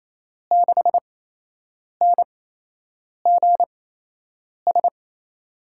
Morse code, heard as a keyed tone, about 28 wpm, spaced as for 8 wpm, 700 Hz. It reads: 6NGS
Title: USA Call Signs - 5 Characters - 8x Character Spacing - 28wpm